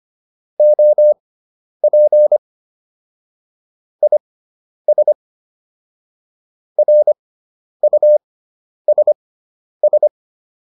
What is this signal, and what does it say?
Morse code, heard as a keyed tone, about 25 wpm, spaced as for 10 wpm, 600 Hz. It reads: OP IS RUSS